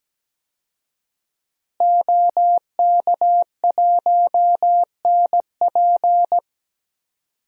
Morse code, heard as a keyed tone, about 17 wpm, 700 Hz. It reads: OK1NP
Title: DXCC-100 Call Signs - 5 Characters - Repeated 3x - 17wpm